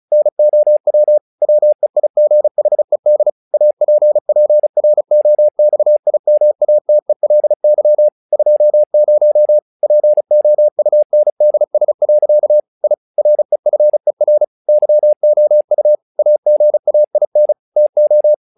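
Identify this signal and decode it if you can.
Morse code, heard as a keyed tone, about 36 wpm, 600 Hz. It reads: NOW WEIGHED APPROXIMATELY 20 POUNDS. I REFER YOU AGAIN TO
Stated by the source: Morse practice text "Tower Trouble" sent at 35 WPM, from the SKCC - Straight Key Century Club learning center audio files.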